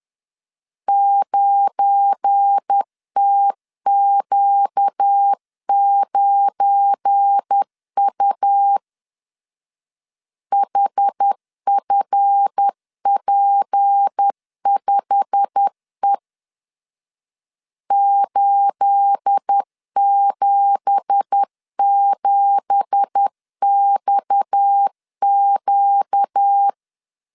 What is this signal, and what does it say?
Morse code, heard as a keyed tone, about 10 wpm, 800 Hz. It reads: 9TQ9U HFP5E 877XQ
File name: CW / DET / 1 9 1500C S B 10wpm 4439